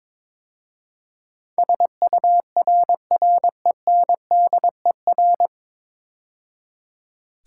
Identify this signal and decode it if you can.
Morse code, heard as a keyed tone, about 22 wpm, 700 Hz. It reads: SURRENDER
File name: 9 Letter Words - First 3 Letters Given - 22wpm